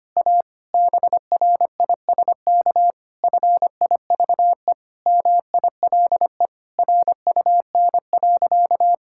Morse code, heard as a keyed tone, about 25 wpm, 700 Hz. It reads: A BRISK FIVE MILE RUN.
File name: practice / 25wpm / Difficult Sentences with Hints - 25wpm